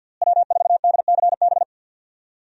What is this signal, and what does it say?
Morse code, heard as a keyed tone, about 52 wpm, 700 Hz. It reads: W4DCB
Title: USA Call Signs - 5 Characters - Repeated 3x - 50wpm